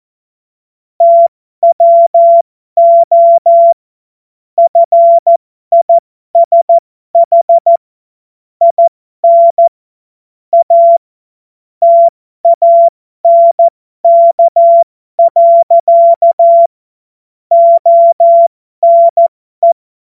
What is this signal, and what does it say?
Morse code, heard as a keyed tone, about 14 wpm, spaced as for 12 wpm, 680 Hz. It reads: TWO FISH IN A TANK. ONE
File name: Morsekurs / Witze